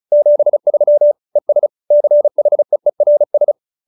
Morse code, heard as a keyed tone, about 34 wpm, 590 Hz. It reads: 73 ES CHEERS